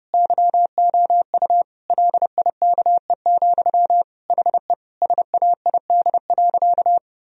Morse code, heard as a keyed tone, about 32 wpm, 700 Hz. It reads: YOU LIKE, HE SAID.
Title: Sentences from Top 1000 Words - Review Part 2 - 30wpm